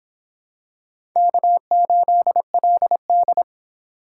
Morse code, heard as a keyed tone, about 23 wpm, 700 Hz. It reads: K8LD